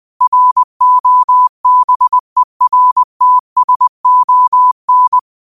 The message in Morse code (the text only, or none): ROBERTSON